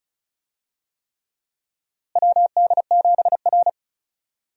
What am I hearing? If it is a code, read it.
Morse code, heard as a keyed tone, about 36 wpm, 700 Hz. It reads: WD7R
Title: USA Call Signs - 4 Characters - Repeated 3x - 35wpm